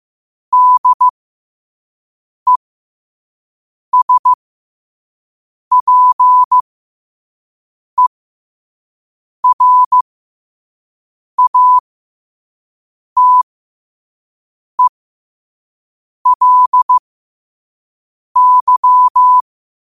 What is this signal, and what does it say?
Morse code, heard as a keyed tone, about 15 wpm, spaced as for 5 wpm, 1000 Hz. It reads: DESPERATELY
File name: qotdfiles / 5wpm